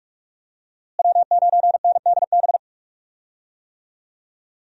Morse code, heard as a keyed tone, about 46 wpm, 700 Hz. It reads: W9NDB
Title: USA Call Signs - 5 Characters - Repeated 3x - 45wpm